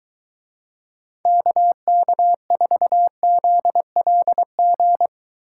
Morse code, heard as a keyed tone, about 23 wpm, 700 Hz. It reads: KK4ZLG